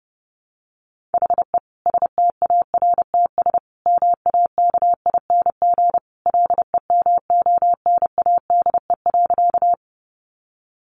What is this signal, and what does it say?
Morse code, heard as a keyed tone, about 30 wpm, 700 Hz. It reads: HE STARTS MAKING LEMONADE.
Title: Additional Easy Phrases 30wpm